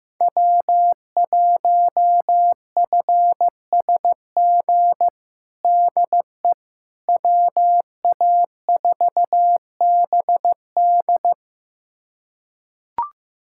Morse code, heard as a keyed tone, about 15 wpm, 700 Hz. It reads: W1FSG DE WA4BD
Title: USA Callsign DE Callsign - 5 Characters - 15wpm